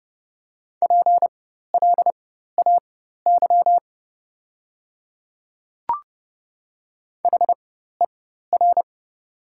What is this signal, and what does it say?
Morse code, heard as a keyed tone, about 29 wpm, spaced as for 14 wpm, 700 Hz. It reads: PLAY HER
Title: 4x Character Spacing - Top 200 Words - 30wpm